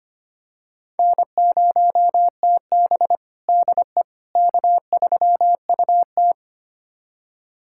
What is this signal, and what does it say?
Morse code, heard as a keyed tone, about 25 wpm, 700 Hz. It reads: N0TB DE K3UT
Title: USA Callsign DE Callsign - 4 Characters - 25wpm